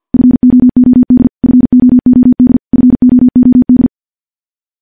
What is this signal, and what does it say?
Morse code, heard as a keyed tone, about 50 wpm, 250 Hz. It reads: FOOD FOOD FOOD
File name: morse / morse-training-50WPM